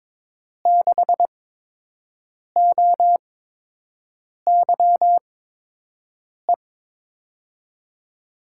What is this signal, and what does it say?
Morse code, heard as a keyed tone, about 22 wpm, spaced as for 6 wpm, 700 Hz. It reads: 6OYE
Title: USA Call Signs - 6 Characters - 8x Character Spacing - 22wpm